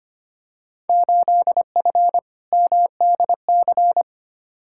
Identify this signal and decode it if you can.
Morse code, heard as a keyed tone, about 25 wpm, 700 Hz. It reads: 8F MDC
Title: Field Day Exchanges Encore - 25wpm